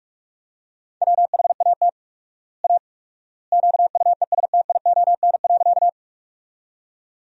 Morse code, heard as a keyed tone, about 43 wpm, 700 Hz. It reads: WHAT A QUESTION.
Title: 4x Word-Spacing - Sentences from Top 400 Words - 45wpm